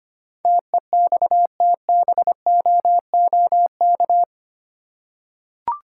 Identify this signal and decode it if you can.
Morse code, heard as a keyed tone, about 25 wpm, 700 Hz. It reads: TEXTBOOK
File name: practice / 25wpm / Easy Words with Hints - 25wpm